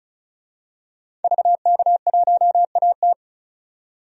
Morse code, heard as a keyed tone, about 36 wpm, 700 Hz. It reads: VK1AT
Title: DXCC-100 Call Signs - 5 Characters - 35wpm